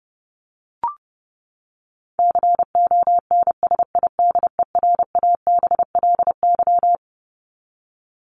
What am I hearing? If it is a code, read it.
Morse code, heard as a keyed tone, about 30 wpm, 700 Hz. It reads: CONSIDERABLY
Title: LY Suffix - Top 100 Words with Base Words 30wpm